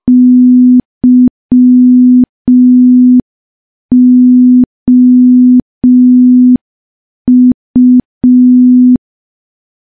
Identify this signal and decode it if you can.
Morse code, heard as a keyed tone, about 5 wpm, 250 Hz. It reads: YOU
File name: morse / morse-training-05WPM